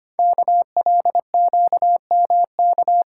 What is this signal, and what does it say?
Morse code, heard as a keyed tone, about 25 wpm, 700 Hz. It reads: KLQMK
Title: Code Groups - 5 Letters - 25wpm